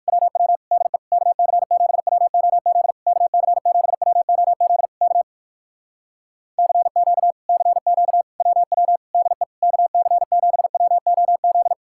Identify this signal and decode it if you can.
Morse code, heard as a keyed tone, about 65 wpm, spaced as for 57 wpm, 700 Hz. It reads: WW DE KC7WOZ KC7WOZ K CQ CQ WW DE KC7WOZ